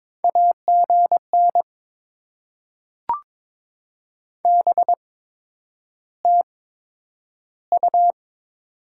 Morse code, heard as a keyed tone, about 22 wpm, spaced as for 9 wpm, 700 Hz. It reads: AGN BTU
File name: Top 20 QSO Elements - 8x Character Spacing - 22wpm